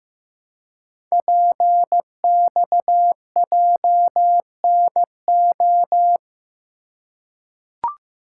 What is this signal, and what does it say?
Morse code, heard as a keyed tone, about 15 wpm, 700 Hz. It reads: PXJNO
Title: Code Groups - 5 Letters - 15wpm